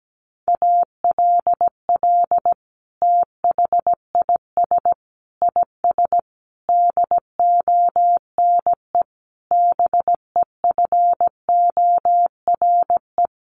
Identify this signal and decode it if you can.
Morse code, heard as a keyed tone, about 17 wpm, 700 Hz. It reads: ALL THIS IS DONE BEFORE
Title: Sentences from Top 1500 Words 17wpm